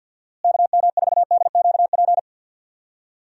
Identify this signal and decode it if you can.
Morse code, heard as a keyed tone, about 49 wpm, 700 Hz. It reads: KM4DQP
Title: USA Call Signs - 6 Characters - Repeated 3x - 50wpm